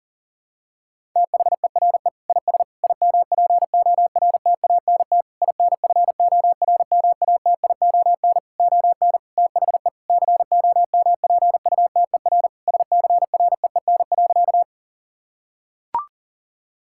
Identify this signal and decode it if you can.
Morse code, heard as a keyed tone, about 37 wpm, 700 Hz. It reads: THERE IS IMPORTANT INFORMATION ON THE COMPUTER SCREEN.
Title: Sentences from Top 1000 Words - Review Part 6 - 40wpm